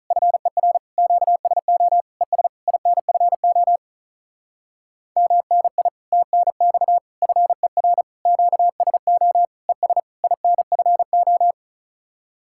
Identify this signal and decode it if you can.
Morse code, heard as a keyed tone, about 42 wpm, 700 Hz. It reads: FER QSO ES INFO MNI TNX FER QSO ES INFO
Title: Speed-Racing Common QSO Phrases - 35wpm